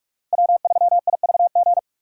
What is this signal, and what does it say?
Morse code, heard as a keyed tone, about 46 wpm, 700 Hz. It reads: W3IVG
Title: USA Call Signs - 2x Character Spacing - 45wpm